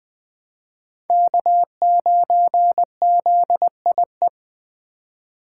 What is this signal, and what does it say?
Morse code, heard as a keyed tone, about 21 wpm, 700 Hz. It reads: K9ZIE
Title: 2023 POTA Activator Call Signs - 20wpm